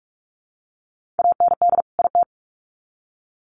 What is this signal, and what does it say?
Morse code, heard as a keyed tone, about 44 wpm, 700 Hz. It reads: AND IT
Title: Sets of 2 Words 45wpm